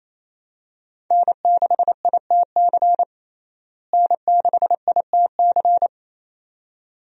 Morse code, heard as a keyed tone, about 28 wpm, 700 Hz. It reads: N6ITC N6ITC
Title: USA Call Signs - 5 Characters - Repeated 2x - 28wpm